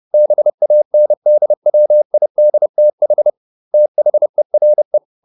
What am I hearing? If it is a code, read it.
Morse code, heard as a keyed tone, about 30 wpm, 600 Hz. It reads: BANDWIDTH THERE